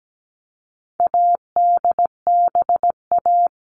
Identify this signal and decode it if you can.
Morse code, heard as a keyed tone, about 17 wpm, 700 Hz. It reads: ADBA
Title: All Cut Numbers - 4-Digit 17wpm